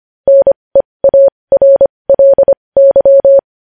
Morse code, heard as a keyed tone, about 25 wpm, spaced as for 20 wpm, 550 Hz. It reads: NEARLY
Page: news-headlines-in-morse-code-at-20-wpm